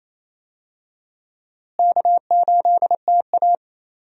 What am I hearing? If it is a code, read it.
Morse code, heard as a keyed tone, about 28 wpm, 700 Hz. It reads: K8TA
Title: USA Call Signs - 4 Characters - Repeated 2x - 28wpm